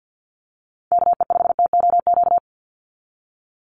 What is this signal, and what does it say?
Morse code, heard as a keyed tone, about 50 wpm, 700 Hz. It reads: KE5TOQ
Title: USA Call Signs 50wpm